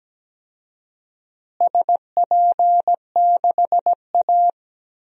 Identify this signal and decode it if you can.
Morse code, heard as a keyed tone, about 17 wpm, 700 Hz. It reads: SP6A
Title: DXCC-100 Call Signs - 4 Characters - Repeated 2x - 17wpm